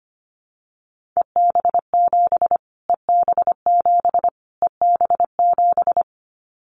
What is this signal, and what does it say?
Morse code, heard as a keyed tone, about 25 wpm, 700 Hz. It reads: EB7 EB7 EB7
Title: Callsign Prefixes for Top 25 DXCC - Repeated 3x - 25wpm